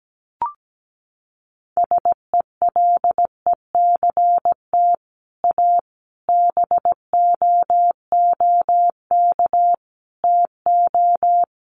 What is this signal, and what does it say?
Morse code, heard as a keyed tone, about 17 wpm, 700 Hz. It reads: SELECT A BOOK TO
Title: Sentences from Top 1300 Words 17wpm